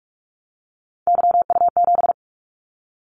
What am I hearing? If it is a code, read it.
Morse code, heard as a keyed tone, about 46 wpm, 700 Hz. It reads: YU7
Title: Callsign Prefixes for Top 25 DXCC - 45wpm